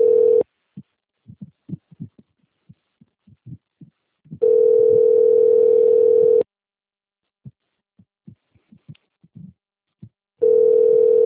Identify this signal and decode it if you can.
Ringback tone (North America), 440 + 480 Hz.